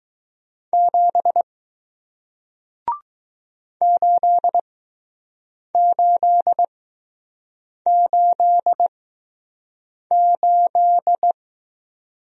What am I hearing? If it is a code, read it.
Morse code, heard as a keyed tone, about 20 wpm, spaced as for 6 wpm, 700 Hz. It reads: 7 8888